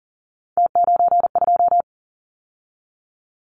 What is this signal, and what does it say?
Morse code, heard as a keyed tone, about 40 wpm, 700 Hz. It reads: T92
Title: Callsign Prefixes for Top 100 DXCC - 40wpm